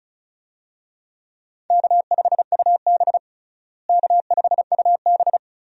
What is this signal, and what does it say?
Morse code, heard as a keyed tone, about 36 wpm, 700 Hz. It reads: K5UB K5UB
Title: USA Call Signs - 4 Characters - Repeated 2x - 35wpm